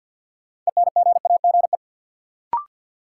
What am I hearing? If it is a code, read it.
Morse code, heard as a keyed tone, about 51 wpm, 700 Hz. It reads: ENGAGE
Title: Sentences from Top 1500 Words - New Word Drill Alt - 50wpm